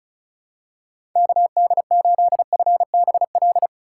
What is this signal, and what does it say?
Morse code, heard as a keyed tone, about 34 wpm, 700 Hz. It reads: KD8FBL
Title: USA Call Signs - 6 Characters - 4x Character Spacing - 35wpm